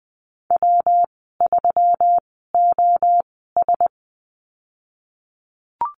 Morse code, heard as a keyed tone, about 20 wpm, spaced as for 15 wpm, 700 Hz. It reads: W3OS
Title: 128 Lesson 21g - Number 6 - Call Signs 20wpm